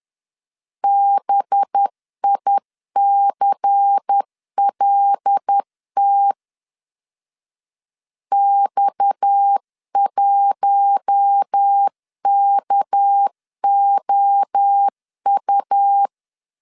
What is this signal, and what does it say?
Morse code, heard as a keyed tone, about 10 wpm, 800 Hz. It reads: BICLT X1KOU
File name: CW / DET / 1 6 1500C S C 10wpm 4252